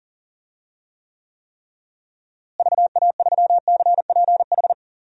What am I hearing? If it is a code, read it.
Morse code, heard as a keyed tone, about 37 wpm, 700 Hz. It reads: VA3CPH
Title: DXCC-100 Call Signs - 6 Characters - Repeated 3x - 40wpm